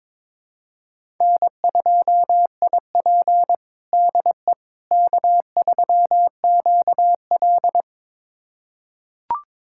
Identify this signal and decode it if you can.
Morse code, heard as a keyed tone, about 22 wpm, 700 Hz. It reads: N2IP DE K3QL
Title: USA Callsign DE Callsign - 4 Characters - 22wpm